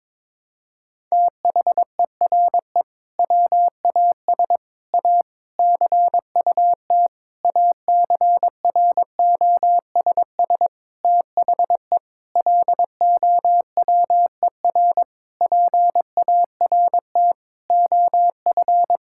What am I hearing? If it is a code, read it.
Morse code, heard as a keyed tone, about 22 wpm, 700 Hz. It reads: THERE WAS A CUT ACROSS THE LOWER PART OF
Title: Sentences from Top 1000 Words - Review Part 5 - 22wpm